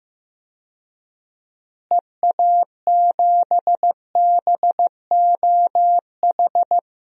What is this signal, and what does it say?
Morse code, heard as a keyed tone, about 16 wpm, 700 Hz. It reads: EA7BOH
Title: DXCC-100 Call Signs - 6 Characters - Repeated 3x - 15wpm